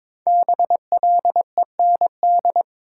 Morse code, heard as a keyed tone, about 22 wpm, 700 Hz. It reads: BLEND
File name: 5 Letter Words - First 3 Letters Given - 22wpm